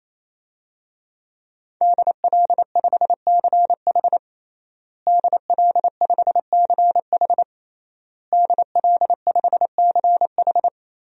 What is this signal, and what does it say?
Morse code, heard as a keyed tone, about 28 wpm, 700 Hz. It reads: DL5CH DL5CH DL5CH
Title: DXCC-100 Call Signs - 5 Characters - Repeated 3x - 28wpm